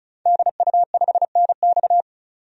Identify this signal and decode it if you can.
Morse code, heard as a keyed tone, about 35 wpm, 700 Hz. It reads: DU5NX
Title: DXCC-100 Call Signs - 5 Characters - 35wpm